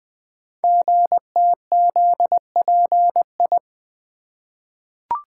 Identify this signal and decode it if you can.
Morse code, heard as a keyed tone, about 20 wpm, 700 Hz. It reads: GTZPI
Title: Code Groups - 5 Letters - 20wpm